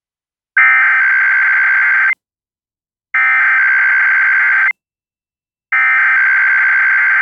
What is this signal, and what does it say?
Emergency Alert System (SAME) tones (header bursts), heard as harsh buzzing bursts. It is heard from 0.6 to 6.4 s.